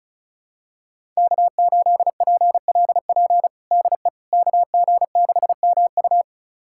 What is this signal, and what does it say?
Morse code, heard as a keyed tone, about 36 wpm, 700 Hz. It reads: K8PLP DE KG6MU
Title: USA Callsign DE Callsign - 5 Characters - 35wpm